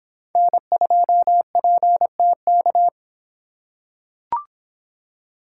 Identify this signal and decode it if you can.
Morse code, heard as a keyed tone, about 25 wpm, 700 Hz. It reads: N2PTK